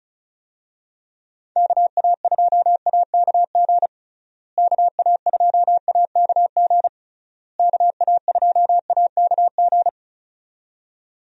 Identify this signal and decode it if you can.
Morse code, heard as a keyed tone, about 34 wpm, 700 Hz. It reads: KA2AKG KA2AKG KA2AKG
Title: USA Call Signs - 6 Characters - Repeated 3x - 35wpm